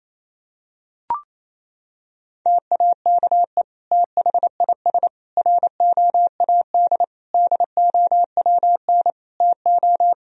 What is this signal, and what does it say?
Morse code, heard as a keyed tone, about 28 wpm, 700 Hz. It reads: TAKE THIS ROAD DOWN TO
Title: Sentences from Top 1000 Words - Review Part 4 - 28wpm